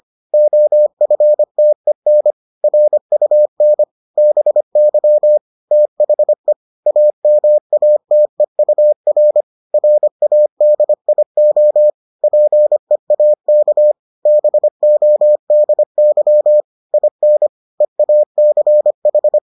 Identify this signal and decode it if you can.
Morse code, heard as a keyed tone, about 25 wpm, 600 Hz. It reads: OFTEN RUN BY THE AMATEUR RADIO PEAK BODY IN EACH